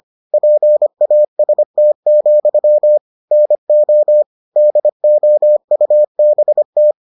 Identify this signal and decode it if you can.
Morse code, heard as a keyed tone, about 25 wpm, 600 Hz. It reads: PAST, NO DOUBT